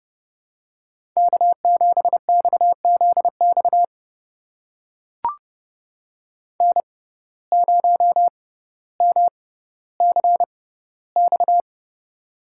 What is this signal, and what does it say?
Morse code, heard as a keyed tone, about 30 wpm, spaced as for 15 wpm, 700 Hz. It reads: K7XZX N0MCX
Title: USA Call Signs - 5 Characters - 6x Character Spacing - 30wpm